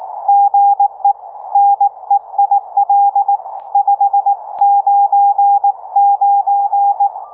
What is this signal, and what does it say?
Morse code, heard as a keyed tone, about 18 wpm, 800 Hz, with radio noise. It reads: GE NEIL 599